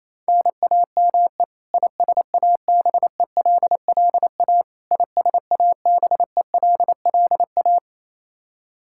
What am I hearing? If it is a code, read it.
Morse code, heard as a keyed tone, about 27 wpm, 700 Hz. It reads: NAME ISABELLA ISABELLA
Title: K1USN SST QSO Practice 28wpm